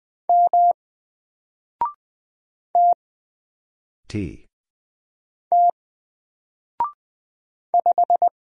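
Morse code, heard as a keyed tone, about 20 wpm, 700 Hz. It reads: M T T 5
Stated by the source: Single letter and number Morse drill at 20 wpm, from N4LCD.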